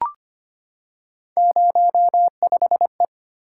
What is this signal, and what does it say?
Morse code, heard as a keyed tone, about 25 wpm, 700 Hz. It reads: 05E
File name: Three Letter-Number Combinations - 25wpm